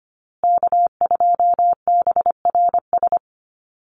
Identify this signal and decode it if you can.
Morse code, heard as a keyed tone, about 25 wpm, 700 Hz. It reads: K2BRS
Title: USA Call Signs Repeated 2x 25wpm